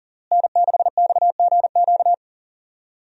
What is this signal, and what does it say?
Morse code, heard as a keyed tone, about 41 wpm, 700 Hz. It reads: N6XGQ
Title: USA Call Signs - 5 Characters - Repeated 3x - 40wpm